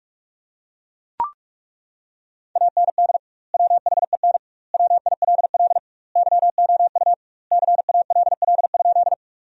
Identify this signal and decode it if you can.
Morse code, heard as a keyed tone, about 41 wpm, 700 Hz. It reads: AND WHEN WILL YOU CALL?
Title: 2x Word-Spacing - Sentences from Top 100 Words - 45wpm